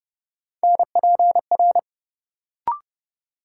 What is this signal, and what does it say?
Morse code, heard as a keyed tone, about 30 wpm, 700 Hz. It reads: NPR
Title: Sets of 5 Words - New Word Drill - 30wpm